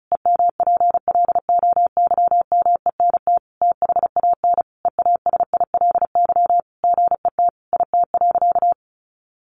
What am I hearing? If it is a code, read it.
Morse code, heard as a keyed tone, about 34 wpm, 700 Hz. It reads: EMPLOYMENT THAN EASILY GET IT.
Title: Sentences from Top 1500 Words 35wpm